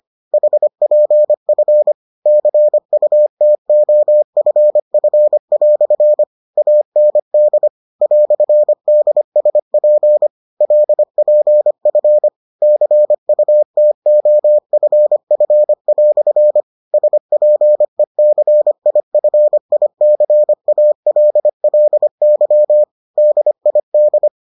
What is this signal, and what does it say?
Morse code, heard as a keyed tone, about 25 wpm, 600 Hz. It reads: HPF CUTOFF" AND "DSP LPF CUTOFF" SPECIFICALLY DID